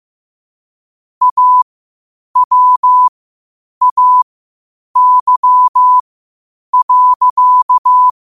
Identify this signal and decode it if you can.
Morse code, heard as a keyed tone, about 15 wpm, spaced as for 8 wpm, 1000 Hz. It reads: AWAY.
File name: qotdfiles / 8wpm